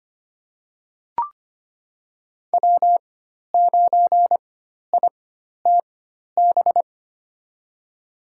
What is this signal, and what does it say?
Morse code, heard as a keyed tone, about 25 wpm, spaced as for 12 wpm, 700 Hz. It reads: W9ITB
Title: USA Call Signs - 4x Character Spacing - 25wpm